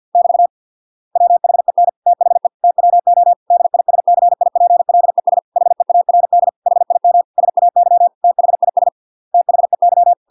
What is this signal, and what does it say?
Morse code, heard as a keyed tone, about 50 wpm, 700 Hz. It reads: <BT> WHEN THE TWO DISCIPLES HEARD HIM SAY THIS THEY